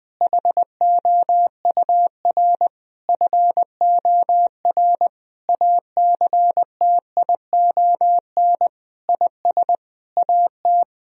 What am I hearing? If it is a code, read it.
Morse code, heard as a keyed tone, about 20 wpm, 700 Hz. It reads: HOUR FOR ACTION IS AT
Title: Sentences from Top 1000 Words - Review Part 4 - 20wpm